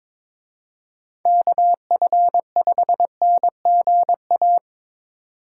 Morse code, heard as a keyed tone, about 26 wpm, 700 Hz. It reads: KF5NGA
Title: Speed Racing - USA 2x3 Call Signs 22wpm